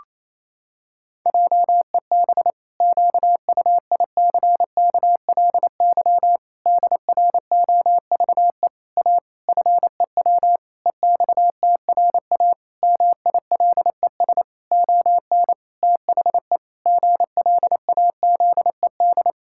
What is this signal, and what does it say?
Morse code, heard as a keyed tone, about 28 wpm, 700 Hz. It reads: JEB QUICKLY DROVE A FEW EXTRA MILES ON THE GLAZED
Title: Pangram Challenge v2 - 28wpm